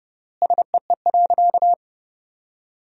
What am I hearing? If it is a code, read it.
Morse code, heard as a keyed tone, about 30 wpm, 700 Hz. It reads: SEE.
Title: 3x Word-Spacing - Sentences from Top 500 Words - 30wpm